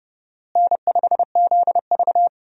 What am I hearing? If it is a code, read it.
Morse code, heard as a keyed tone, about 30 wpm, 700 Hz. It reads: N5ZV
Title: USA Call Signs - 4 Characters - Repeated 3x - 30wpm